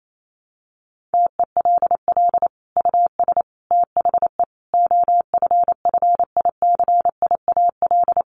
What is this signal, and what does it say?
Morse code, heard as a keyed tone, about 28 wpm, 700 Hz. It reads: TELL US THE OFFICIAL
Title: Sentences from Top 1400 Words 28wpm